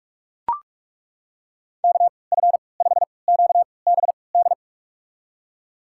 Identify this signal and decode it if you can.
Morse code, heard as a keyed tone, about 41 wpm, spaced as for 25 wpm, 690 Hz. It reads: KF5QBD
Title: USA Call Signs - 6 Characters - 3x Character Spacing - 45wpm